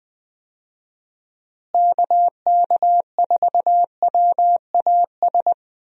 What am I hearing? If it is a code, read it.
Morse code, heard as a keyed tone, about 20 wpm, 700 Hz. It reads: KK4WAS
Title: USA Call Signs - 6 Characters - Repeated 3x - 20wpm